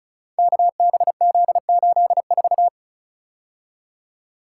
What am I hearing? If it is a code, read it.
Morse code, heard as a keyed tone, about 36 wpm, 700 Hz. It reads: KBZ84
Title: Code Groups - 5 Letter:Numbers - 35wpm